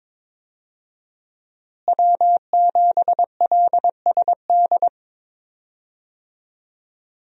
Morse code, heard as a keyed tone, about 22 wpm, 700 Hz. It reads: W7LSD
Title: USA Call Signs - 5 Characters - Repeated 3x - 22wpm